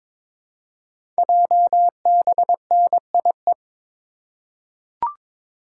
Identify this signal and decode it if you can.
Morse code, heard as a keyed tone, about 23 wpm, 700 Hz. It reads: JBNIE
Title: Code Groups - 5 Letters - 22wpm